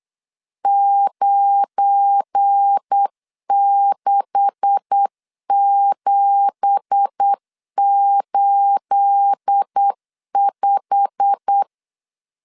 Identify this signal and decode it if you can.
Morse code, heard as a keyed tone, about 8 wpm, 800 Hz. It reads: 96785
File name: LEZIONI / 9 2867 8wpm 3440